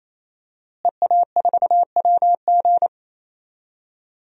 Morse code, heard as a keyed tone, about 28 wpm, 700 Hz. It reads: EA4WG
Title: DXCC-100 Call Signs - 5 Characters - Repeated 3x - 28wpm